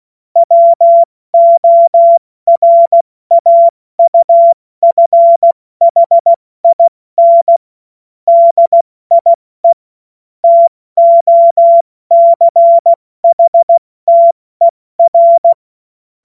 Morse code, heard as a keyed tone, about 16 wpm, 680 Hz. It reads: WORAUFHIN DIE TOCHTER
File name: Morsekurs / Witze / Witz 30. Mai 2023 70 F80